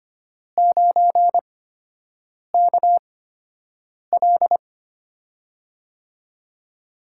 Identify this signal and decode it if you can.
Morse code, heard as a keyed tone, about 25 wpm, spaced as for 7 wpm, 700 Hz. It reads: 9KL
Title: USA Call Signs - 5 Characters - 8x Character Spacing - 25wpm